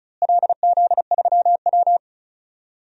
Morse code, heard as a keyed tone, about 36 wpm, 700 Hz. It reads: LZ3W
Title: DXCC-100 Call Signs - 4 Characters - Repeated 3x - 35wpm